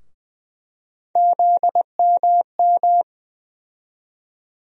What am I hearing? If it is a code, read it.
Morse code, heard as a keyed tone, about 20 wpm, 700 Hz. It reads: ZMM
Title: USA Call Sign Postfixes - 3 Letters 20wpm